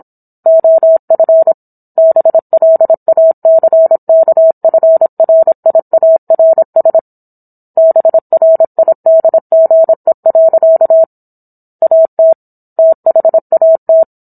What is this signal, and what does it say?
Morse code, heard as a keyed tone, about 27 wpm, 650 Hz. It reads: OF BLACKFRIARS BRIDGE. AT THAT